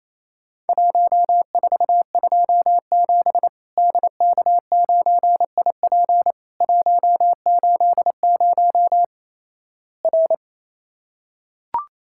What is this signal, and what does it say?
Morse code, heard as a keyed tone, about 28 wpm, 700 Hz. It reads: 1427 DK9IP 180 R
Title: Worked All Europe DX Contest - QTC Exchange - 28wpm